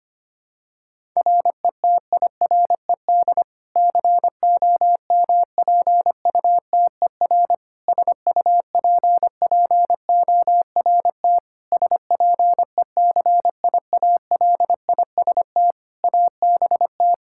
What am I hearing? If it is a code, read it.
Morse code, heard as a keyed tone, about 25 wpm, 700 Hz. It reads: RETIRED COMPUTER SUPPORT SPECIALIST ABT
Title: Practice QSO - Exchange 3 - More Info - 25wpm